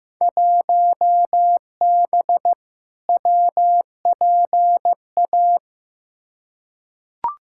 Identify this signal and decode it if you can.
Morse code, heard as a keyed tone, about 15 wpm, 700 Hz. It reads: 1B WPA
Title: Field Day Exchanges - 15wpm